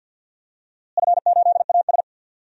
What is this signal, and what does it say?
Morse code, heard as a keyed tone, about 47 wpm, 700 Hz. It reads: F8AS
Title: DXCC-100 Call Signs - 4 Characters - Repeated 3x - 50wpm